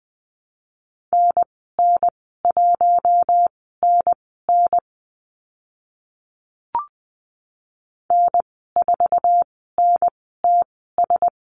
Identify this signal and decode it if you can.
Morse code, heard as a keyed tone, about 20 wpm, spaced as for 14 wpm, 700 Hz. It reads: NN1NN N4NTS
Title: 048 Lesson 9g - Number 4 - Call Signs 20wpm